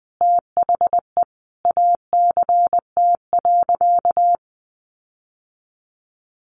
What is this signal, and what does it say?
Morse code, heard as a keyed tone, about 20 wpm, 700 Hz. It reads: THE ACT.
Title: Sentences from Top 1300 Words 20wpm